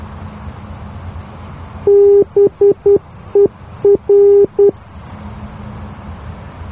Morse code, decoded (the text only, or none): BER